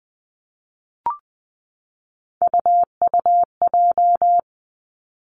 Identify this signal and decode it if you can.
Morse code, heard as a keyed tone, about 20 wpm, 700 Hz. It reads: UUJ
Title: Three Challenging Letter-Number Combinations 20wpm